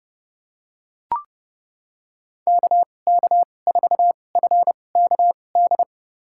Morse code, heard as a keyed tone, about 30 wpm, spaced as for 22 wpm, 700 Hz. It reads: KK4FKD
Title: USA Call Signs - 2x Character Spacing - 30wpm